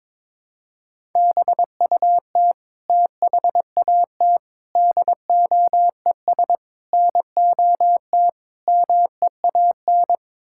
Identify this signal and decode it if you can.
Morse code, heard as a keyed tone, about 22 wpm, 700 Hz. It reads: BUT THAT DOES NOT MEAN
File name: practice / 22wpm / Build A Phrase - 5-Word Phrases - 22wpm